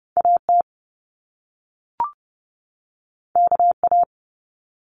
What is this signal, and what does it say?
Morse code, heard as a keyed tone, about 30 wpm, 700 Hz. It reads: AT KA
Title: Top Two Letter Combinations in Words 30wpm